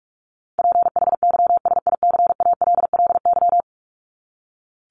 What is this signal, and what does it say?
Morse code, heard as a keyed tone, about 44 wpm, 700 Hz. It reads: PHYSICALLY
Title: Top 1501-1750 Words 45wpm